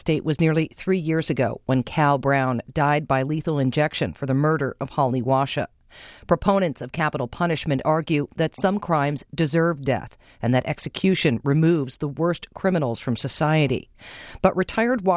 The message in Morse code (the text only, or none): none